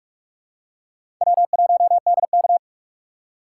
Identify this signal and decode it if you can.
Morse code, heard as a keyed tone, about 46 wpm, 700 Hz. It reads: W1DK
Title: USA Call Signs - 4 Characters - 4x Character Spacing - 45wpm